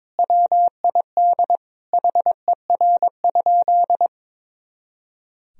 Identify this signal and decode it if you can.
Morse code, heard as a keyed tone, about 22 wpm, 700 Hz. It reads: WID HER?
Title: Sentences from Top 400 Words and Abbreviations 22wpm